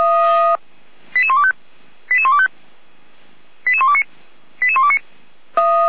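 Five-tone selective call (ZVEI), heard as quick runs of five tones, tones of about 80 ms. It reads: 88125 88125 88129 88129
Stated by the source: ZVEI five-tone selective call received on an SDR, from the Feuerwehr Orlamünde fire brigade.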